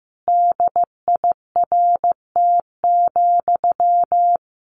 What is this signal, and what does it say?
Morse code, heard as a keyed tone, about 14 wpm, 700 Hz. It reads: DIRT,
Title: A Sound Of Thunder 15wpm